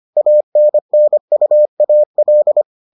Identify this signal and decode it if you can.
Morse code, heard as a keyed tone, about 25 wpm, 600 Hz. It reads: ANNUAL